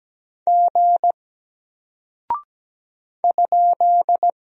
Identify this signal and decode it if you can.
Morse code, heard as a keyed tone, about 17 wpm, 700 Hz. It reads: G ?
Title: Single Character (Letter:Number:Punctuation:Prosign) Alt 17wpm